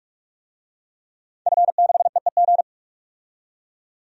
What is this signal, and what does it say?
Morse code, heard as a keyed tone, about 46 wpm, 700 Hz. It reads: F6EEG
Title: DXCC-100 Call Signs - 5 Characters - 45wpm